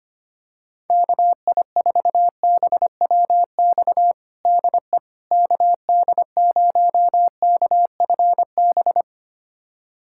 Morse code, heard as a keyed tone, about 25 wpm, 700 Hz. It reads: KI4BWX DE KD0KFB
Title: USA Callsign DE Callsign - 6 Characters - 25wpm